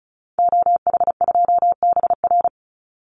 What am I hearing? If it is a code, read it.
Morse code, heard as a keyed tone, about 34 wpm, 700 Hz. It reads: OH2BR
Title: Call Signs - 35wpm